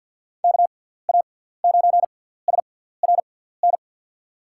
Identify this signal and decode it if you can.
Morse code, heard as a keyed tone, about 49 wpm, 700 Hz. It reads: K A 9 S R N
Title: USA Call Signs - 6 Characters - 6x Character Spacing - 50wpm